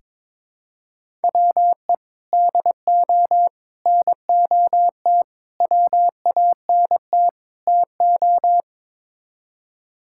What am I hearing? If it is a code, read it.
Morse code, heard as a keyed tone, about 22 wpm, 700 Hz. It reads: WE DO NOT WANT TO
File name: practice / 22wpm / Build A Phrase - 5-Word Phrases - 22wpm